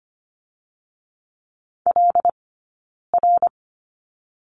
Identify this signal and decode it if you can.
Morse code, heard as a keyed tone, about 25 wpm, 700 Hz. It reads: L R
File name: Lesson 15d - Number 5 - Two Characters 25wpm